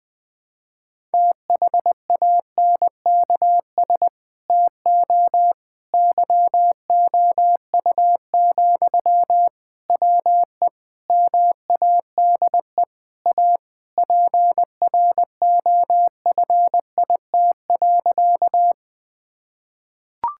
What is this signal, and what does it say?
Morse code, heard as a keyed tone, about 21 wpm, 700 Hz. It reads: THANKS TO YOU, WE MADE A PROFIT.
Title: Sentences from Top 1000 Words - Review Part 5 - 20wpm